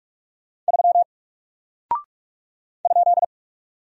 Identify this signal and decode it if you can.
Morse code, heard as a keyed tone, about 44 wpm, 700 Hz. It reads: 3 ?
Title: Single Character (Letter:Number:Punctuation:Prosign) Alt 45wpm